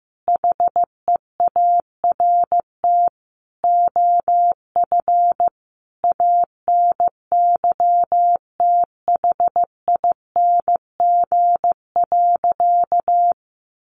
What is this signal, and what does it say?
Morse code, heard as a keyed tone, about 15 wpm, 700 Hz. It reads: HEART OF ANYTHING.
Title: Sentences from Top 1200 Words 15wpm